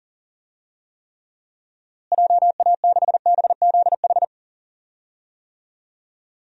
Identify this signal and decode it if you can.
Morse code, heard as a keyed tone, about 37 wpm, 700 Hz. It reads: JA6BZH
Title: DXCC-100 Call Signs - 6 Characters - Repeated 2x - 40wpm